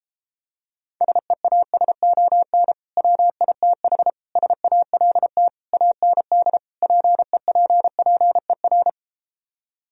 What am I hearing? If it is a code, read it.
Morse code, heard as a keyed tone, about 33 wpm, 700 Hz. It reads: SEASON WITH SALT AND PEPPER